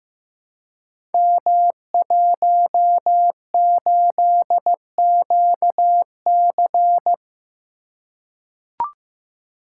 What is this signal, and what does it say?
Morse code, heard as a keyed tone, about 15 wpm, 700 Hz. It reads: M18QC
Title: Code Groups - 5 Letter:Numbers - 15wpm